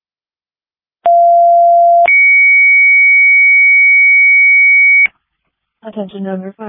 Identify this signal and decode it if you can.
Two-tone page, heard as a one-second tone, then a three-second tone, about 690 then 2250 Hz, with speech. It is heard from 1.0 to 5.1 s.